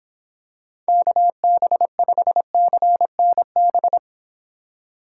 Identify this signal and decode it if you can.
Morse code, heard as a keyed tone, about 25 wpm, 700 Hz. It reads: KB5CNB